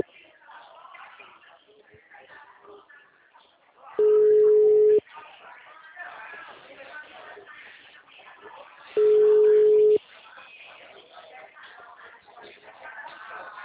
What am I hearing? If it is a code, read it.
Ringback tone (Europe, 425 Hz), with speech.